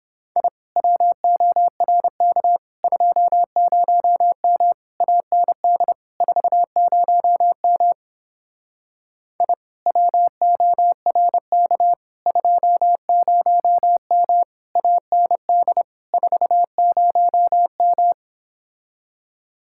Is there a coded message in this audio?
Yes: Morse code from 0.4 to 18.1 s, heard as a keyed tone, about 29 wpm, 700 Hz. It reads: I WORK 20M AND 40M I WORK 20M AND 40M